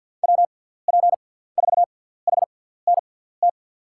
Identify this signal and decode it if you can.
Morse code, heard as a keyed tone, about 52 wpm, spaced as for 17 wpm, 700 Hz. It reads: WP4HNT
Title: USA Call Signs - 6 Characters - 6x Character Spacing - 50wpm